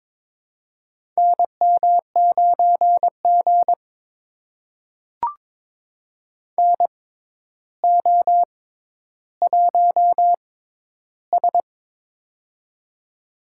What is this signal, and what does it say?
Morse code, heard as a keyed tone, about 21 wpm, spaced as for 11 wpm, 700 Hz. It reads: NM9G NO1S
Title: USA Call Signs - 4 Characters - 6x Character Spacing - 22wpm